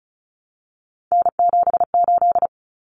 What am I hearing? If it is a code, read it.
Morse code, heard as a keyed tone, about 34 wpm, 700 Hz. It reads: N78
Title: Common Cut Numbers - 3-Digit 35wpm